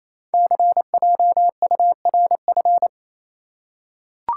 Morse code, heard as a keyed tone, about 27 wpm, 700 Hz. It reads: CJURF
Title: Code Groups - 5 Letters - 28wpm